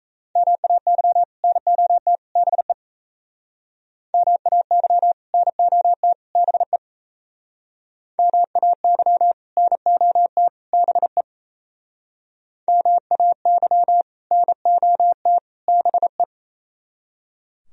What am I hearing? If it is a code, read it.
Morse code, heard as a keyed tone, about 36 wpm, 700 Hz. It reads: MAY NOT BE MAY NOT BE MAY NOT BE MAY NOT BE